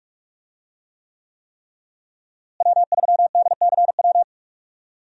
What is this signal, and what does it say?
Morse code, heard as a keyed tone, about 46 wpm, 700 Hz. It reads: W3DCW
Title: USA Call Signs - 5 Characters - Repeated 3x - 45wpm